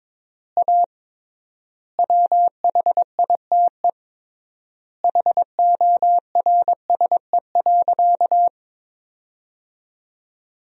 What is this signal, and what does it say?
Morse code, heard as a keyed tone, about 21 wpm, 700 Hz. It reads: A WHITE HORSE.
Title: 3x Word-Spacing - Sentences from Top 300 Words - 22wpm